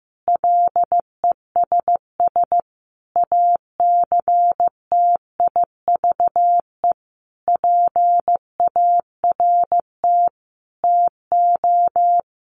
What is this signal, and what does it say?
Morse code, heard as a keyed tone, about 15 wpm, 700 Hz. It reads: LESS ACTIVE PART TO
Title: Sentences from Top 900 Words 15wpm